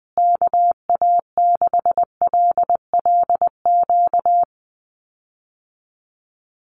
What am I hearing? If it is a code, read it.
Morse code, heard as a keyed tone, about 20 wpm, 700 Hz. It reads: KA6LLQ
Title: USA Call Signs Repeated 3x 20wpm